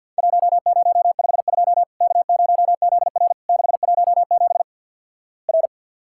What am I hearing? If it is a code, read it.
Morse code, heard as a keyed tone, about 50 wpm, 700 Hz. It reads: 1052 K0ZR 617 R
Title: Worked All Europe DX Contest - QTC Exchange - 50wpm